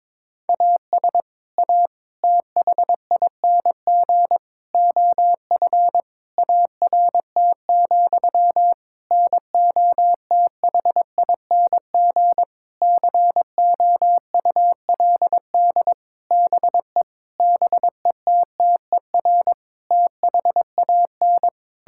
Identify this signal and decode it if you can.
Morse code, heard as a keyed tone, about 22 wpm, 700 Hz. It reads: AS A THING OF ART, NOTHING COULD BE BETTER THAN
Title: Sentences from Top 1000 Words - Review Part 6 - 22wpm